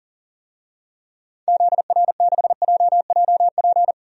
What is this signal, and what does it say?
Morse code, heard as a keyed tone, about 40 wpm, 700 Hz. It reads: ZR6JJP
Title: DXCC-100 Call Signs - 6 Characters - 40wpm